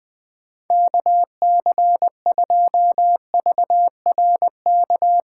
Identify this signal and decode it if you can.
Morse code, heard as a keyed tone, about 19 wpm, 700 Hz. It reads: KC2VRK
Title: Speed Racing - USA 2x3 Call Signs 17wpm